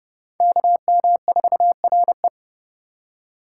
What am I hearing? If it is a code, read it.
Morse code, heard as a keyed tone, about 28 wpm, 700 Hz. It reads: KM4RE